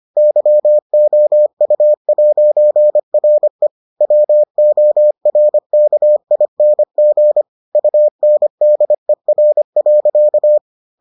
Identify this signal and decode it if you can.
Morse code, heard as a keyed tone, about 25 wpm, 600 Hz. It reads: YOU'RE WORKING UNDER.